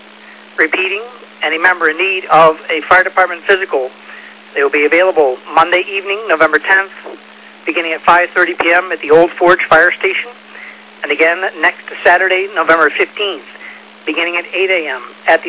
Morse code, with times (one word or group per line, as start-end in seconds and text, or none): none